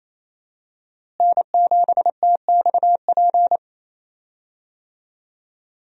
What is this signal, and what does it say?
Morse code, heard as a keyed tone, about 28 wpm, 700 Hz. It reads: N7TXP
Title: USA Call Signs - 5 Characters - Repeated 2x - 28wpm